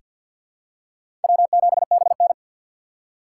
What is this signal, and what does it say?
Morse code, heard as a keyed tone, about 47 wpm, 700 Hz. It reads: W7BN
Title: USA Call Signs - 4 Characters - 3x Character Spacing - 50wpm